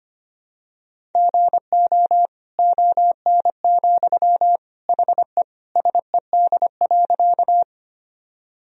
Morse code, heard as a keyed tone, about 25 wpm, 700 Hz. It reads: GO ON, HE SED.
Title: Sentences from Top 100 Words and Abbreviations 25wpm